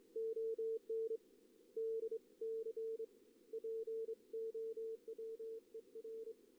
Morse code, heard as a keyed tone, about 26 wpm, 450 Hz. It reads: ON DC POWER
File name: contests / CW-SS-k5tr- -1828